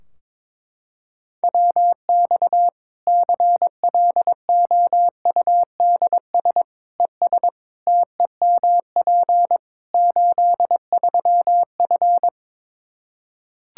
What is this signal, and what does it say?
Morse code, heard as a keyed tone, about 21 wpm, 700 Hz. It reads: WX CLOUDS ES TEMP 83F
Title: K1USN SST QSO Practice 22wpm